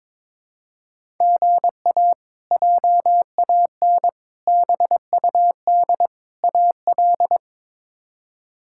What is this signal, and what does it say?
Morse code, heard as a keyed tone, about 22 wpm, 700 Hz. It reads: GA JAN BUD AL
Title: K1USN SST QSO Practice - 22wpm